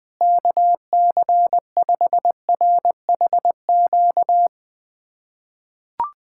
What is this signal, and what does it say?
Morse code, heard as a keyed tone, about 20 wpm, 700 Hz. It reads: KC5RHQ
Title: USA Callsign DE Callsign 20wpm